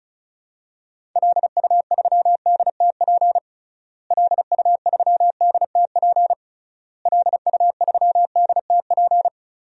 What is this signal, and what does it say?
Morse code, heard as a keyed tone, about 34 wpm, 700 Hz. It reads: LU3DTP LU3DTP LU3DTP
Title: DXCC-100 Call Signs - 6 Characters - Repeated 3x - 35wpm